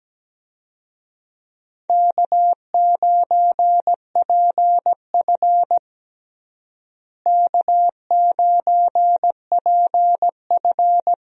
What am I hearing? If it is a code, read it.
Morse code, heard as a keyed tone, about 17 wpm, 700 Hz. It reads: K9PF K9PF
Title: USA Call Signs - 4 Characters - Repeated 3x - 17wpm